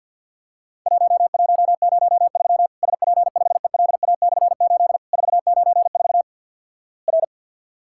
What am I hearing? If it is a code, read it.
Morse code, heard as a keyed tone, about 50 wpm, 700 Hz. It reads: 1103 SP5ELA/8 494 R
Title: Worked All Europe DX Contest - QTC Exchange - 50wpm